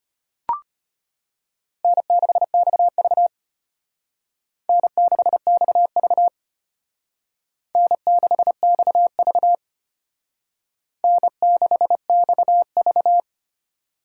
Morse code, heard as a keyed tone, about 32 wpm, 700 Hz. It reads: N6XV N6XV N6XV N6XV